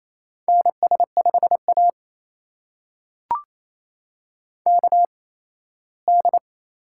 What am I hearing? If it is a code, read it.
Morse code, heard as a keyed tone, about 28 wpm, spaced as for 17 wpm, 700 Hz. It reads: NS5A KD
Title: USA Call Signs - 4 Characters - 8x Character Spacing - 28wpm